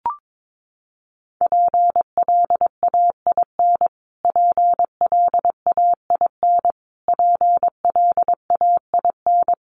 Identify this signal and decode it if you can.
Morse code, heard as a keyed tone, about 22 wpm, 700 Hz. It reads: PLAIN PLAIN PLAIN
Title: Top 500 Words Repeated 22wpm